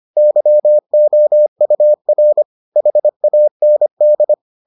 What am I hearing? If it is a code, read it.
Morse code, heard as a keyed tone, about 25 wpm, 600 Hz. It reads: YOUR HAND